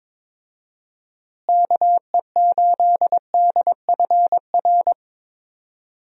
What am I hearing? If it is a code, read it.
Morse code, heard as a keyed tone, about 22 wpm, 700 Hz. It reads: KE8DFR
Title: USA Call Signs - 6 Characters - Repeated 3x - 22wpm